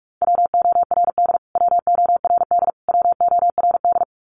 Morse code, heard as a keyed tone, about 44 wpm, 700 Hz. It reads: WORD WORD WORD
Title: Top 500 Words Repeated 45wpm